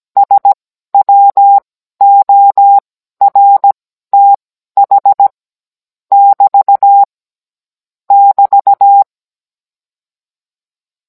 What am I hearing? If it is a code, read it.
Morse code, heard as a keyed tone, about 17 wpm, spaced as for 12 wpm, 800 Hz. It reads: SWORTH <BT> <BT>